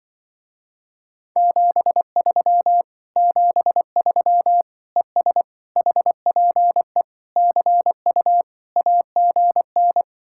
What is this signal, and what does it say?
Morse code, heard as a keyed tone, about 24 wpm, 700 Hz. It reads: 73 73 ES HPE CU AGN
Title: Speed-Racing Common QSO Phrases - 20wpm